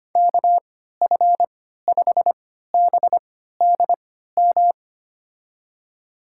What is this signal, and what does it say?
Morse code, heard as a keyed tone, about 25 wpm, spaced as for 14 wpm, 700 Hz. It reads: KF5BDM
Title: USA Call Signs - 6 Characters - 3x Character Spacing - 25wpm